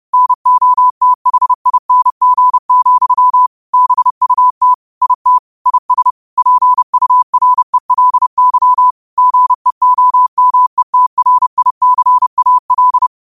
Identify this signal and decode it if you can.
Morse code, heard as a keyed tone, about 30 wpm, 1000 Hz. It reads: NOTHING, BUT IT IS PURELY GEOMETRICAL
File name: 30wpm